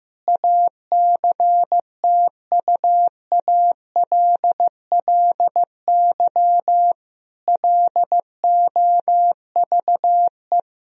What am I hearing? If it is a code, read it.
Morse code, heard as a keyed tone, about 15 wpm, 700 Hz. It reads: ACTUALLY LOVE
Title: Sentences from Top 1000 Words - Review Part 4 - 15wpm